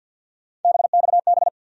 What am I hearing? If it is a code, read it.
Morse code, heard as a keyed tone, about 53 wpm, 700 Hz. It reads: BXB
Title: USA Call Sign Postfixes - 3 Letters 50wpm